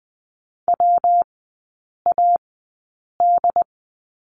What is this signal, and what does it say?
Morse code, heard as a keyed tone, about 20 wpm, spaced as for 8 wpm, 700 Hz. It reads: WAD
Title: 121 Lesson 20e - Number 3 - Three Characters 20wpm